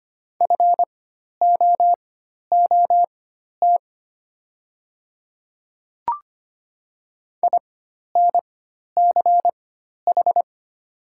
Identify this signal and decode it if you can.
Morse code, heard as a keyed tone, about 25 wpm, spaced as for 12 wpm, 700 Hz. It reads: FOOT INCH
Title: 4x Character Spacing - Top 500 Words - 25wpm